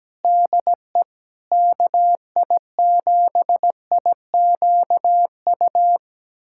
Morse code, heard as a keyed tone, about 17 wpm, 700 Hz. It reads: DE KI7IQU
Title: USA Callsign DE Callsign - 6 Characters - 17wpm